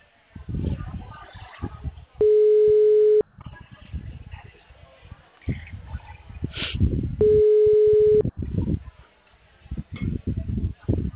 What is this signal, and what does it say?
Ringback tone (Europe, 425 Hz).
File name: out- -2006- - - .48940